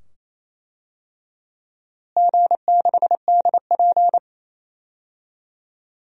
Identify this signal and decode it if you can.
Morse code, heard as a keyed tone, about 28 wpm, 700 Hz. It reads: G6DP
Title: DXCC-100 Call Signs - 4 Characters - Repeated 2x - 28wpm